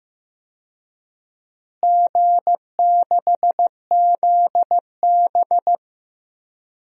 Morse code, heard as a keyed tone, about 15 wpm, 700 Hz. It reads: G6ZB
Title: DXCC-100 Call Signs - 4 Characters - Repeated 3x - 15wpm